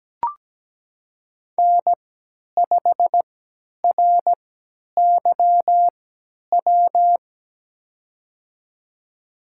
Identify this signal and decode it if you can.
Morse code, heard as a keyed tone, about 17 wpm, spaced as for 10 wpm, 700 Hz. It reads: N5RYW
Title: USA Call Signs - 5 Characters - 3x Character Spacing - 17wpm